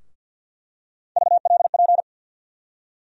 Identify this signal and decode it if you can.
Morse code, heard as a keyed tone, about 47 wpm, 700 Hz. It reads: VLP
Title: USA Call Sign Postfixes - 3 Letters 50wpm